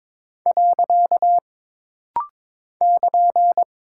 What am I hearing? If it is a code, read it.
Morse code, heard as a keyed tone, about 22 wpm, 700 Hz. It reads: . <KN>
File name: practice / 22wpm / Single Punctuation-Prosign - Rapid-Fire - 22wpm